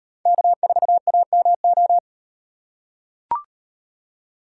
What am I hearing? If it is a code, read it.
Morse code, heard as a keyed tone, about 38 wpm, 700 Hz. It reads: K4AMO